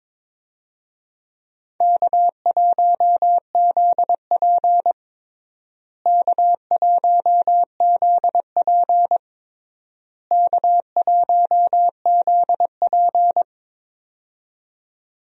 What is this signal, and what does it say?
Morse code, heard as a keyed tone, about 22 wpm, 700 Hz. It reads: K1ZP K1ZP K1ZP
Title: USA Call Signs - 4 Characters - Repeated 3x - 22wpm